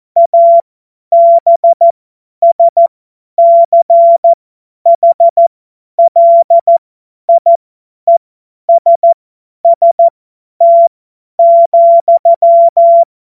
Morse code, heard as a keyed tone, about 14 wpm, spaced as for 10 wpm, 680 Hz. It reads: ABSCHLIESST,
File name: Witz-16-September-2025-50-F70